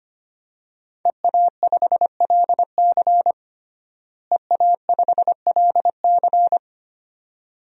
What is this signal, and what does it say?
Morse code, heard as a keyed tone, about 25 wpm, 700 Hz. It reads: EA5LC EA5LC
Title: DXCC-100 Call Signs - 5 Characters - Repeated 2x - 25wpm